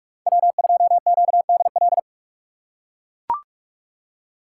Morse code, heard as a keyed tone, about 42 wpm, 700 Hz. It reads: W2QDL
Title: USA Call Signs - 5 Characters - 2x Character Spacing - 45wpm